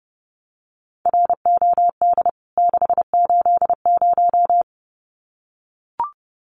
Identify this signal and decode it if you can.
Morse code, heard as a keyed tone, about 30 wpm, 700 Hz. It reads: ROD 680
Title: CWT Exchange 30wpm